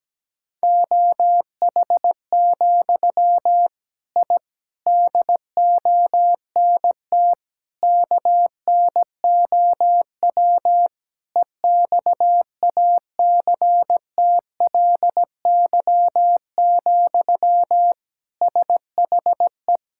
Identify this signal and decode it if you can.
Morse code, heard as a keyed tone, about 17 wpm, 700 Hz. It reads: OH, I DONT KNOW EXACTLY, SHE
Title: Sentences from Top 1000 Words - Review Part 6 - 17wpm